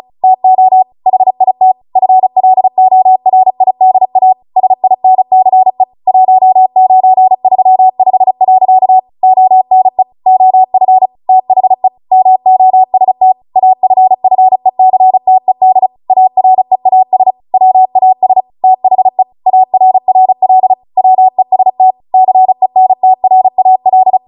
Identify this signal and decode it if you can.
Morse code, heard as a keyed tone, about 36 wpm, 750 Hz. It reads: TO HIT FLORIDA SINCE 1935. ONE OF THE MOST AFFECTED AREAS WAS THE ARRL WEST CENTRAL